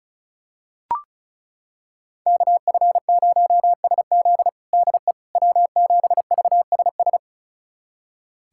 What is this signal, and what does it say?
Morse code, heard as a keyed tone, about 36 wpm, 700 Hz. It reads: KF0SZ DE W7VSS
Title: USA Callsign DE Callsign - 5 Characters - 35wpm